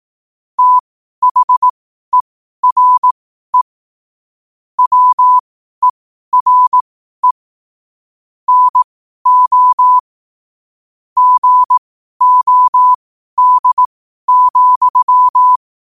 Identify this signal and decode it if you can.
Morse code, heard as a keyed tone, about 18 wpm, spaced as for 12 wpm, 1000 Hz. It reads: THERE WERE NO GOD,
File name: qotdfiles / 12wpm